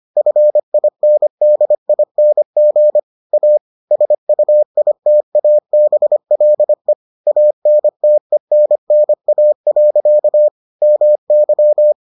Morse code, heard as a keyed tone, about 25 wpm, 600 Hz. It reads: FINDING A SUITABLE ANTENNA. MY